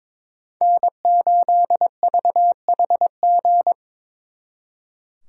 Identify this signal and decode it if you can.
Morse code, heard as a keyed tone, about 26 wpm, 700 Hz. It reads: N8VHG
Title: Speed Racing - USA 1x3 Call Signs 22wpm